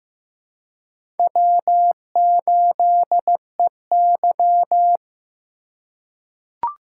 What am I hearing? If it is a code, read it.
Morse code, heard as a keyed tone, about 15 wpm, 700 Hz. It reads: W8EY
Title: USA Call Signs - 4 Characters - 3x Character Spacing - 15wpm